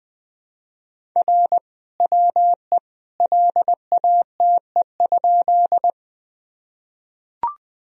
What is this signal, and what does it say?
Morse code, heard as a keyed tone, about 20 wpm, 700 Hz. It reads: R WE LATE?
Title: Sentences from Top 300 Words and Abbreviations 20wpm